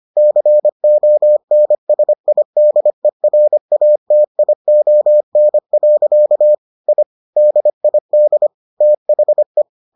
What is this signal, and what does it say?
Morse code, heard as a keyed tone, about 25 wpm, 600 Hz. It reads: CONSIDERATION. I DID THE